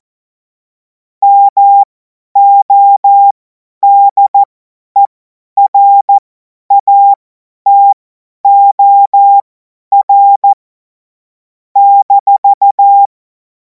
Morse code, heard as a keyed tone, about 14 wpm, spaced as for 10 wpm, 800 Hz. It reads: MODERATOR -
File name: Morsekurs / Witze / Witz 12. Mai 2026 50 F70